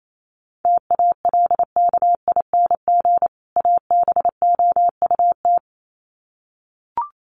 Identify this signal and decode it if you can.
Morse code, heard as a keyed tone, about 28 wpm, 700 Hz. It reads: TALKING ABOUT
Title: Sets of 2 Words 28wpm